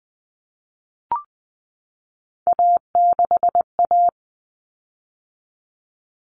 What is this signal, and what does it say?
Morse code, heard as a keyed tone, about 20 wpm, 700 Hz. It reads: A6A
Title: Common Cut Numbers - 3-Digit 20wpm